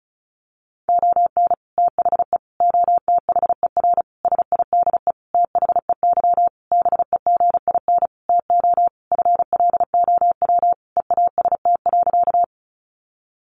Morse code, heard as a keyed tone, about 34 wpm, 700 Hz. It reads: ON THE OTHER SIDE THEY BEGIN TO FLOW EAST.
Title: Sentences from Top 1200 Words 35wpm